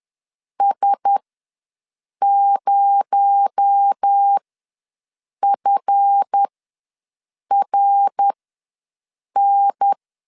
Morse code, heard as a keyed tone, about 10 wpm, spaced as for 6 wpm, 800 Hz. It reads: S0FRN
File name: LEZIONI / 7 GF9Z / 7 GF9Z ws 10wpm 3919